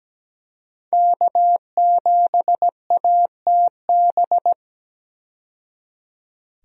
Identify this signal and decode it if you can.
Morse code, heard as a keyed tone, about 17 wpm, 700 Hz. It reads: K7ATB
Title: USA Call Signs - 5 Characters - 17wpm